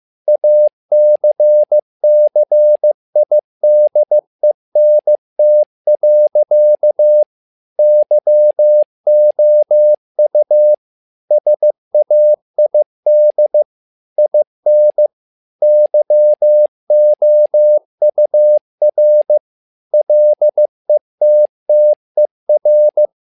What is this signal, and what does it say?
Morse code, heard as a keyed tone, about 15 wpm, 600 Hz. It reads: ACCIDENT. YOU SAID IN YOUR LETTER
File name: learning center / tower trouble / tower trouble 15wpm